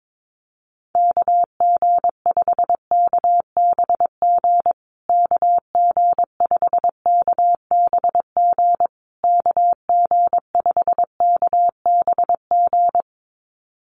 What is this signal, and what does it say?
Morse code, heard as a keyed tone, about 22 wpm, 690 Hz. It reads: KG5KBG KG5KBG KG5KBG
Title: USA Call Signs Repeated 3x 22wpm